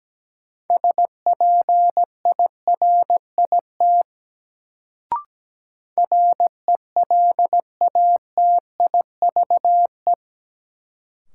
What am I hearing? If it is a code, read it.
Morse code, heard as a keyed tone, about 17 wpm, 700 Hz. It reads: SPIRIT RELATIVE
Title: Sentences from Top 1400 Words - New Word Drill - 17wpm